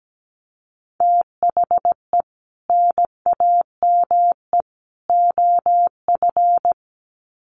Morse code, heard as a keyed tone, about 17 wpm, 700 Hz. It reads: THE NAME OF
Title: Sets of 3 Words 17wpm